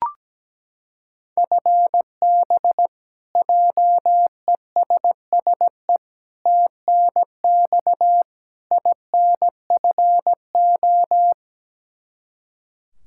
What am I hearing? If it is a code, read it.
Morse code, heard as a keyed tone, about 17 wpm, 700 Hz. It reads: FB JESSE TNX INFO
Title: Practice QSO - Exchange 3 - More Info - 17wpm